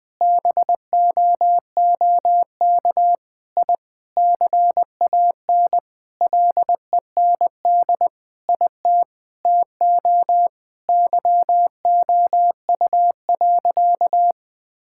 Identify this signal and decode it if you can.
Morse code, heard as a keyed tone, about 20 wpm, 700 Hz. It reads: BOOK I CAN LEND IT TO YOU.
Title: Sentences with 5-Word Phrases - Spoken Beginning - 20wpm